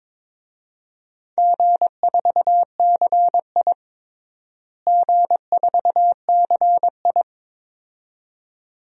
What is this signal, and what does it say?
Morse code, heard as a keyed tone, about 21 wpm, 700 Hz. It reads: G4CI G4CI
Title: DXCC-100 Call Signs - 4 Characters - Repeated 2x - 22wpm